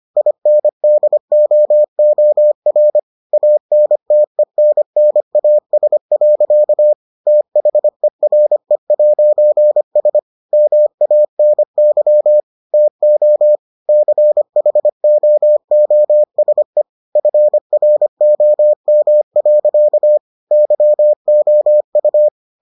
Morse code, heard as a keyed tone, about 25 wpm, 600 Hz. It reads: INDOOR ANTENNAS. THERE'S MANY TO CHOOSE FROM. YOU